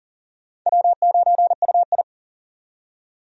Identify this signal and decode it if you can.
Morse code, heard as a keyed tone, about 39 wpm, 700 Hz. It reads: W9UI
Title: USA Call Signs - 4 Characters - 40wpm